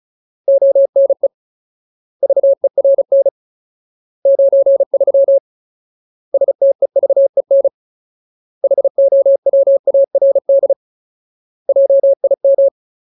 Morse code, heard as a keyed tone, about 34 wpm, 550 Hz. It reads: ONE VERN 93 STEVEN HOWARD JIM
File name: Ultimate Word List / 35WPM Ultimate